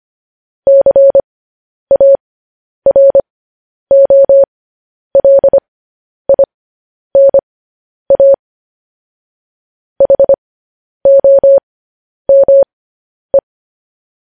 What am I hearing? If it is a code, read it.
Morse code, heard as a keyed tone, about 25 wpm, spaced as for 10 wpm, 550 Hz. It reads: CAROLINA HOME